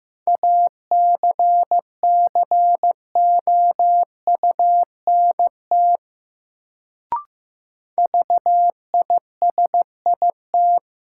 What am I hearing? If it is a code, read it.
Morse code, heard as a keyed tone, about 15 wpm, 700 Hz. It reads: ACCOUNT VISIT
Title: Sentences from Top 900 Words - New Word Drill - 15wpm